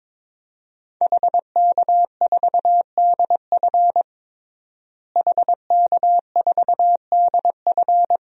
Morse code, heard as a keyed tone, about 22 wpm, 700 Hz. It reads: HK4DF HK4DF
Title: DXCC-100 Call Signs - 5 Characters - Repeated 2x - 22wpm